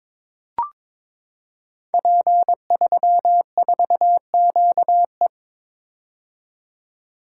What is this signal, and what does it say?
Morse code, heard as a keyed tone, about 23 wpm, 700 Hz. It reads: P34QE
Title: Code Groups - 5 Letter:Numbers - 22wpm